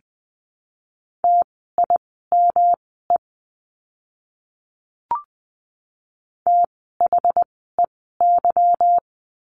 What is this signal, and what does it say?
Morse code, heard as a keyed tone, about 20 wpm, spaced as for 14 wpm, 700 Hz. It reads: TIME THEY
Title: 185 Lesson 30f - Letter b - Words 20wpm